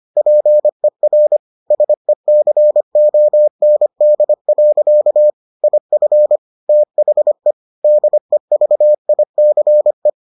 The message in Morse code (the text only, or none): PER SECOND. IF THE DEVICE